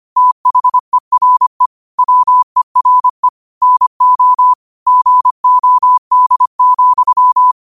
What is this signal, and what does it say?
Morse code, heard as a keyed tone, about 25 wpm, 1000 Hz. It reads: THERE WERE NO GOD,